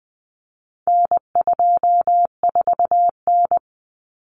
Morse code, heard as a keyed tone, about 20 wpm, 700 Hz. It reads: N24N
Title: Common Cut Numbers - 4-Digit 20wpm